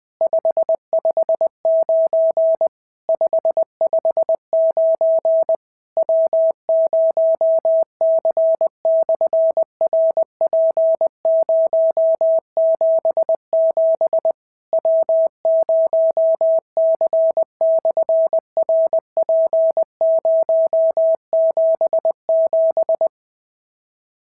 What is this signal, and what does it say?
Morse code, heard as a keyed tone, about 20 wpm, 650 Hz. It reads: 559 559 W0C/RP077 W0C/RP077
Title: SOTA QSO Practice 20wpm